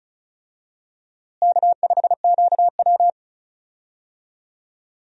Morse code, heard as a keyed tone, about 34 wpm, 700 Hz. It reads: K5QW
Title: USA Call Signs - 4 Characters - Repeated 3x - 35wpm